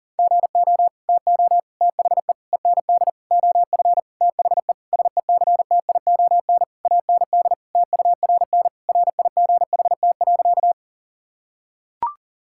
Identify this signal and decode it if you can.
Morse code, heard as a keyed tone, about 40 wpm, 700 Hz. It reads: GO TO THE END OF THE SECTION AND TURN RIGHT.
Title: Sentences from Top 1000 Words - Review Part 5 - 40wpm